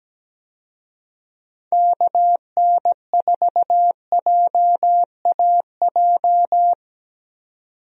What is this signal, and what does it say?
Morse code, heard as a keyed tone, about 17 wpm, 700 Hz. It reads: KN4JAJ
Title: USA Call Signs - 6 Characters - 17wpm